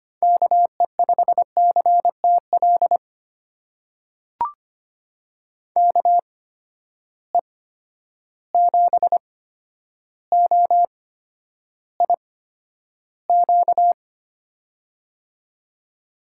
Morse code, heard as a keyed tone, about 25 wpm, spaced as for 11 wpm, 700 Hz. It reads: KE5CTL KE7OIQ
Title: USA Call Signs - 6 Characters - 8x Character Spacing - 25wpm